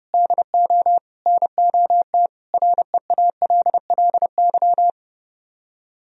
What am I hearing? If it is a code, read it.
Morse code, heard as a keyed tone, about 31 wpm, 700 Hz. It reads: DO NOT REALLY